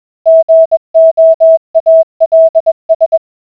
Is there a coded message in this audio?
Yes: Morse code from 0.3 to 3.2 s, heard as a keyed tone, about 21 wpm, 650 Hz. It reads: GOALS